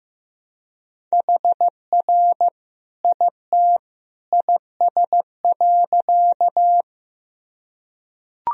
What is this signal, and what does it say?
Morse code, heard as a keyed tone, about 15 wpm, 700 Hz. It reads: HR IT IS.
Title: Sentences from Top 200 Words and Abbreviations 15wpm